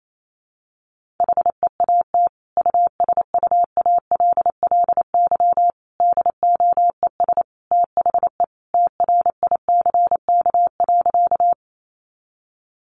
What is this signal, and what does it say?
Morse code, heard as a keyed tone, about 28 wpm, 700 Hz. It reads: HEAT USUALLY DOES THE TRICK.
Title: Sentences from Top 1500 Words 28wpm